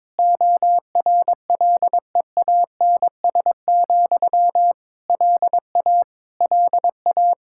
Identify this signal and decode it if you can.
Morse code, heard as a keyed tone, about 22 wpm, 700 Hz. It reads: ORLEANS, LA LA
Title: Practice QSO - Exchange 1 - Beginning - 22wpm